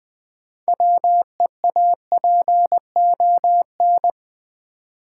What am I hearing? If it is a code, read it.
Morse code, heard as a keyed tone, about 20 wpm, 700 Hz. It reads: WEAPON
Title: Build A Word - 3 Letters At A Time - 6-Letter Words - 20wpm